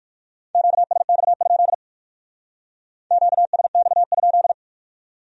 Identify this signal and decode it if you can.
Morse code, heard as a keyed tone, about 46 wpm, 700 Hz. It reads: QSX? QSX?
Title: Speed-Racing Q-Codes - 45wpm